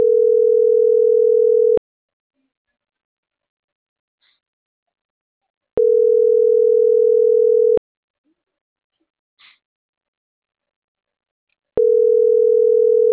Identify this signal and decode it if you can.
Ringback tone (North America), 440 + 480 Hz.